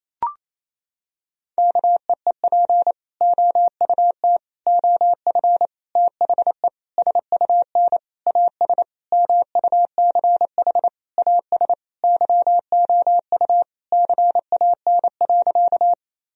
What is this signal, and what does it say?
Morse code, heard as a keyed tone, about 28 wpm, 700 Hz. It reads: KEEP OUT OF THE SUN AS MUCH AS YOU CAN.
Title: Sentences from Top 1000 Words - Review Part 1 - 28wpm